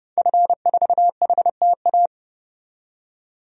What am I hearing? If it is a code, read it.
Morse code, heard as a keyed tone, about 30 wpm, 700 Hz. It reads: F4HTA
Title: DXCC-100 Call Signs - 5 Characters - 30wpm